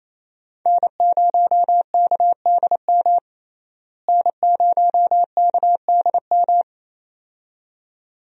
Morse code, heard as a keyed tone, about 28 wpm, 700 Hz. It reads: N0KDM N0KDM
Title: USA Call Signs - 5 Characters - Repeated 2x - 28wpm